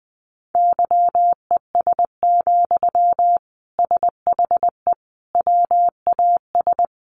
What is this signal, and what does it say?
Morse code, heard as a keyed tone, about 20 wpm, 700 Hz. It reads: YES, SHE WAS
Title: Sentences from Top 500 Words 20wpm